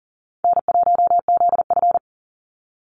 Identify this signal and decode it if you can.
Morse code, heard as a keyed tone, about 40 wpm, 700 Hz. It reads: N1ZF
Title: NA SOTA Activator Callsigns 40wpm